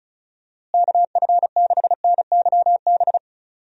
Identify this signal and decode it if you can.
Morse code, heard as a keyed tone, about 34 wpm, 700 Hz. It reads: KF6NYB
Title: USA Call Signs - 6 Characters - Repeated 3x - 35wpm